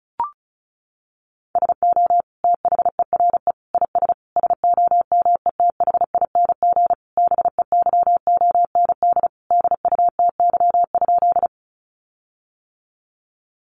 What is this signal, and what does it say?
Morse code, heard as a keyed tone, about 34 wpm, 700 Hz. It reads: SO THERE IS SOMETHING BEYOND DUTY?
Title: Sentences from Top 1500 Words 35wpm